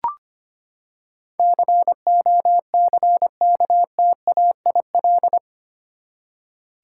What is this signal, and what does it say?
Morse code, heard as a keyed tone, about 25 wpm, 700 Hz. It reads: COCKTAIL
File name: Eight Letter Words - 25wpm